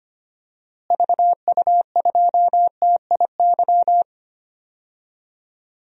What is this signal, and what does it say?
Morse code, heard as a keyed tone, about 25 wpm, 700 Hz. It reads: VU2TIY
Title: DXCC-100 Call Signs - 6 Characters - Repeated 3x - 25wpm